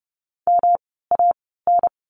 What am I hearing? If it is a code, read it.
Morse code, heard as a keyed tone, about 30 wpm, spaced as for 17 wpm, 700 Hz. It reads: MAN
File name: course / 30:10wpm / 108 Lesson 18f - Letter m - Words 30wpm